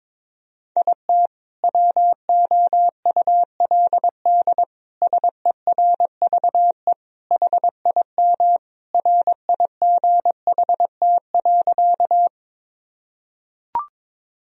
Morse code, heard as a keyed tone, about 22 wpm, 700 Hz. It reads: IT WOULD SERVE HIM RIGHT.
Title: Sentences from Top 1000 Words - Review Part 2 - 22wpm